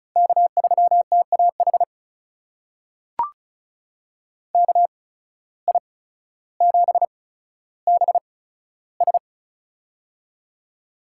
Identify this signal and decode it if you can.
Morse code, heard as a keyed tone, about 36 wpm, spaced as for 15 wpm, 700 Hz. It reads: K3TAH KI7BS
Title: USA Call Signs - 5 Characters - 8x Character Spacing - 35wpm